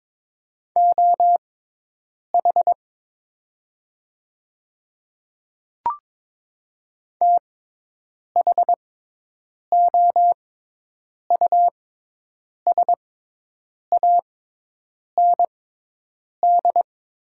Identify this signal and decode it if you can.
Morse code, heard as a keyed tone, about 23 wpm, spaced as for 8 wpm, 700 Hz. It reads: OH THOUSAND
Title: 6x Character Spacing - Top 500 Words - 22wpm